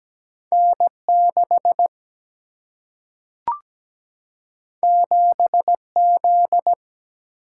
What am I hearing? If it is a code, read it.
Morse code, heard as a keyed tone, about 17 wpm, 700 Hz. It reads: N6 7Z
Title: Single-Letter Single-Number 17wpm